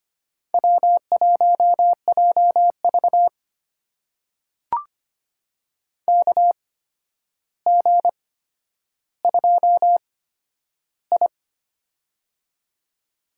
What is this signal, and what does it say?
Morse code, heard as a keyed tone, about 25 wpm, spaced as for 11 wpm, 700 Hz. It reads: W1JV KG2I
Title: USA Call Signs - 4 Characters - 8x Character Spacing - 25wpm